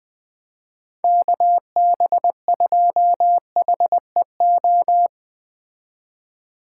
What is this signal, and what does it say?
Morse code, heard as a keyed tone, about 20 wpm, 700 Hz. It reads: KB2HEO